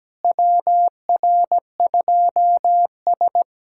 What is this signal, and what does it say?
Morse code, heard as a keyed tone, about 17 wpm, 700 Hz. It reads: WR2S
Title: USA Callsign DE Callsign - 4 Characters - 17wpm